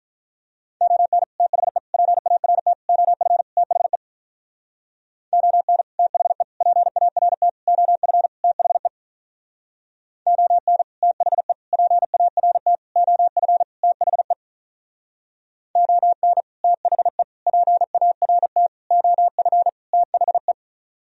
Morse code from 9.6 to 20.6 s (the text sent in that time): ON THE PART OF THE ON THE PART OF THE